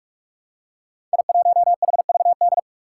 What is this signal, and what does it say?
Morse code, heard as a keyed tone, about 44 wpm, 700 Hz. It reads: I1HVD
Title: Five Challenging Letter-Number Combinations 45wpm